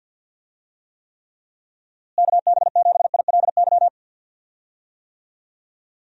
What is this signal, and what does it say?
Morse code, heard as a keyed tone, about 52 wpm, 700 Hz. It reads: KB7ILY
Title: USA Call Signs - 6 Characters - Repeated 3x - 50wpm